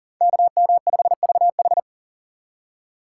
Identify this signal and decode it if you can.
Morse code, heard as a keyed tone, about 40 wpm, 690 Hz. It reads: KM5VH
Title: USA Call Signs - 5 Characters - 3x Character Spacing - 40wpm